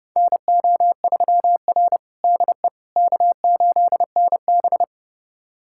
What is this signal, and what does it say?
Morse code, heard as a keyed tone, about 32 wpm, 700 Hz. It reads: NO3R DE K8NB
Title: USA Callsign DE Callsign - 4 Characters - 30wpm